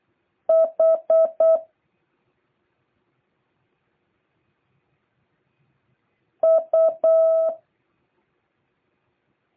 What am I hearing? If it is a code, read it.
Morse code, heard as a keyed tone, about 8 wpm, 650 Hz. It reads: H U